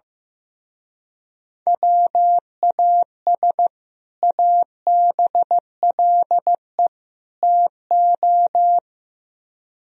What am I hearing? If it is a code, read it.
Morse code, heard as a keyed tone, about 16 wpm, 700 Hz. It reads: WAS ABLE TO
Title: Speed-Racing - Sets of 3 Words 15wpm